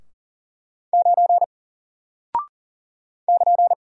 Morse code, heard as a keyed tone, about 40 wpm, 700 Hz. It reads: 9 <KN>
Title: Single Character (Letter:Number:Punctuation:Prosign) 40wpm